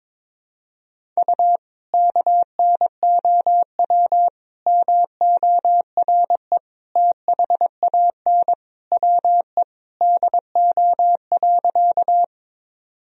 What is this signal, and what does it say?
Morse code, heard as a keyed tone, about 22 wpm, 700 Hz. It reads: U KNOW MORE THAN WE DO.
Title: Sentences from Top 100 Words and Abbreviations 22wpm